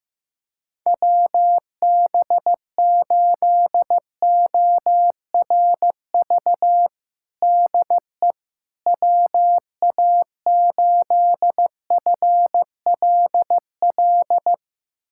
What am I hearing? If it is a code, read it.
Morse code, heard as a keyed tone, about 15 wpm, 700 Hz. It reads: WB8ORV DE WA8FLL
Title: USA Callsign DE Callsign - 6 Characters - 15wpm